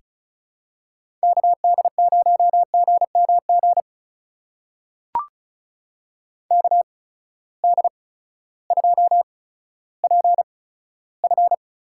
Morse code, heard as a keyed tone, about 36 wpm, spaced as for 16 wpm, 700 Hz. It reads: KD0GMG KD2PF
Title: USA Call Signs - 6 Characters - 8x Character Spacing - 35wpm